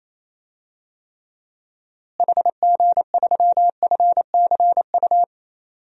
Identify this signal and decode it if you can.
Morse code, heard as a keyed tone, about 28 wpm, 700 Hz. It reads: HG3FCU
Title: DXCC-100 Call Signs - 6 Characters - Repeated 3x - 28wpm